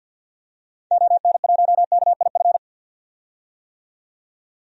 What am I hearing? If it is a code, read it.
Morse code, heard as a keyed tone, about 50 wpm, 700 Hz. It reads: ON1KIF
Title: DXCC-100 Call Signs - 6 Characters - 50wpm